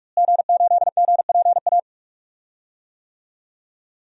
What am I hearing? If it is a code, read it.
Morse code, heard as a keyed tone, about 41 wpm, 700 Hz. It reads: G8GPA
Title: DXCC-100 Call Signs - 5 Characters - Repeated 2x - 45wpm